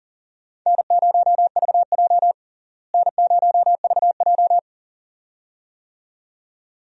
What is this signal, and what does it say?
Morse code, heard as a keyed tone, about 40 wpm, 700 Hz. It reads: N0VJ N0VJ
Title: USA Call Signs - 4 Characters - Repeated 2x - 40wpm